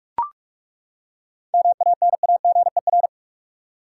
Morse code, heard as a keyed tone, about 42 wpm, 700 Hz. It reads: MANAGER
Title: Seven Letter Words - 45wpm